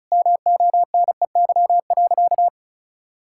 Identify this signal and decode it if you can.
Morse code, heard as a keyed tone, about 36 wpm, 700 Hz. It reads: MONEY.
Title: Sentences from Top 1000 Words - Review Part 6 - 35wpm